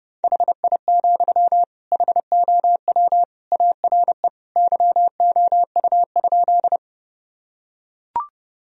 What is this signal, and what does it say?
Morse code, heard as a keyed tone, about 29 wpm, 700 Hz. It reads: HI, HOW ARE YOU?
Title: Sentences from Top 1000 Words - Review Part 3 - 30wpm